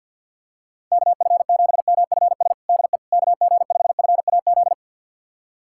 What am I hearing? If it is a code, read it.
Morse code, heard as a keyed tone, about 51 wpm, 700 Hz. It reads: KF7MFS DE KG5FAZ
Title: USA Callsign DE Callsign - 6 Characters - 50wpm